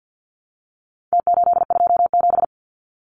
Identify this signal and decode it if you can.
Morse code, heard as a keyed tone, about 50 wpm, 700 Hz. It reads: T827
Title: Numbers - Contest Style - 4-Digit 50wpm